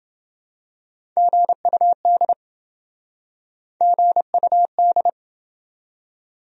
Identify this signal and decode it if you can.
Morse code, heard as a keyed tone, about 27 wpm, 700 Hz. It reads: GUD GUD